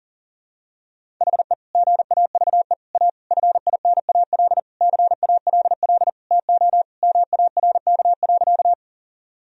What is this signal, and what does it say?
Morse code, heard as a keyed tone, about 40 wpm, 700 Hz. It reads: HE GAVE A FINAL CALL TO MARK.
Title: Sentences from Top 1000 Words - Review Part 2 - 40wpm